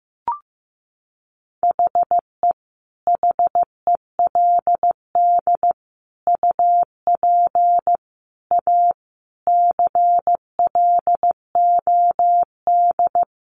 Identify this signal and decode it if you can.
Morse code, heard as a keyed tone, about 14 wpm, 700 Hz. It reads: HE HELD UP A CLOD
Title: A Sound Of Thunder 15wpm